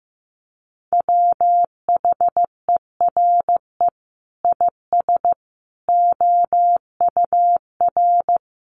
Morse code, heard as a keyed tone, about 15 wpm, 700 Hz. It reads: WHERE IS OUR
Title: Sentences from Top 1300 Words 15wpm